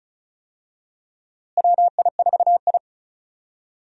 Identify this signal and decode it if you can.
Morse code, heard as a keyed tone, about 34 wpm, 700 Hz. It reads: WI4I
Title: USA Call Signs - 4 Characters - Repeated 3x - 35wpm